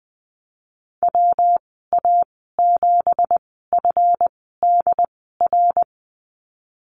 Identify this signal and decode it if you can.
Morse code, heard as a keyed tone, about 20 wpm, spaced as for 15 wpm, 700 Hz. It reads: WA7FDR
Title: 193 Lesson 31g - Letter v - Call Signs 20wpm